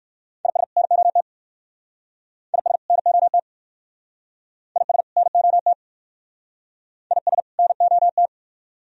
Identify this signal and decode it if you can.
Morse code, heard as a keyed tone, about 61 wpm, 700 Hz. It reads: IS NOT IS NOT IS NOT IS NOT